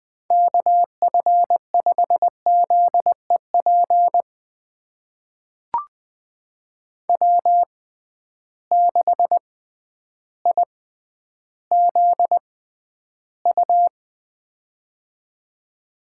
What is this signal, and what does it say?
Morse code, heard as a keyed tone, about 20 wpm, spaced as for 11 wpm, 700 Hz. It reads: KF5ZEP W6IZU
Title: USA Call Signs - 6x Character Spacing - 20wpm